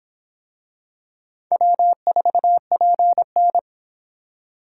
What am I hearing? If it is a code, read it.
Morse code, heard as a keyed tone, about 25 wpm, 700 Hz. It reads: W4PN